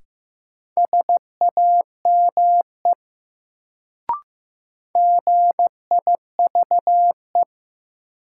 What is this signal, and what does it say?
Morse code, heard as a keyed tone, about 15 wpm, 700 Hz. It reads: SAME GIVE
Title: Sentences from Top 200 Words - New Word Drill - 15wpm